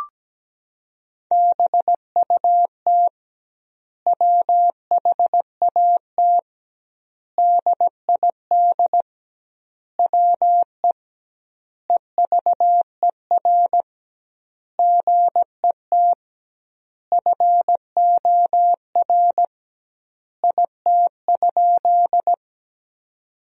Morse code, heard as a keyed tone, about 17 wpm, 700 Hz. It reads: BUT WHAT DID WE EVER GET FOR IT?
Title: 2x Word-Spacing - Sentences from Top 400 Words - 17wpm